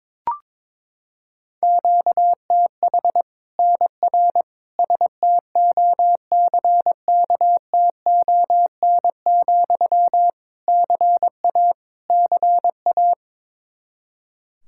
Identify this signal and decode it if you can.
Morse code, heard as a keyed tone, about 22 wpm, 700 Hz. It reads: QTH NR STOCKTON, CA CA
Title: Practice QSO - Exchange 1 - Beginning - 22wpm